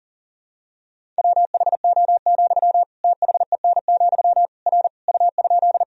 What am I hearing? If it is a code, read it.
Morse code, heard as a keyed tone, about 40 wpm, 700 Hz. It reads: WHO, THEN, R U?
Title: Sentences from Top 100 Words and Abbreviations 40wpm